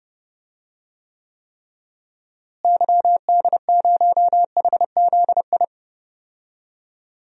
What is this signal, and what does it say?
Morse code, heard as a keyed tone, about 30 wpm, 700 Hz. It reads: YD0HZI
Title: DXCC-100 Call Signs - 6 Characters - Repeated 3x - 30wpm